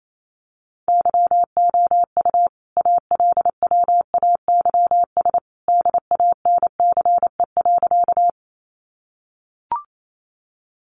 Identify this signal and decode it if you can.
Morse code, heard as a keyed tone, about 28 wpm, 700 Hz. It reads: YOU ALWAYS DANCE.
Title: Sentences from Top 700 Words 28wpm